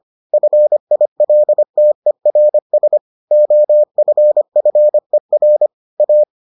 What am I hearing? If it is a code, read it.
Morse code, heard as a keyed tone, about 25 wpm, 600 Hz. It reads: FILTERS OFFER A